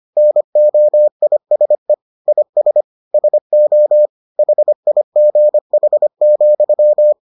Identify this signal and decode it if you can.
Morse code, heard as a keyed tone, about 25 wpm, 600 Hz. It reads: NOISE IS SO HIGH,